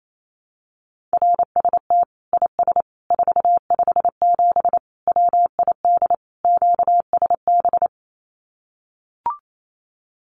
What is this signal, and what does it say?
Morse code, heard as a keyed tone, about 28 wpm, 700 Hz. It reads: RST IS 457 WID QSB
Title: Common QSO Phrases 28wpm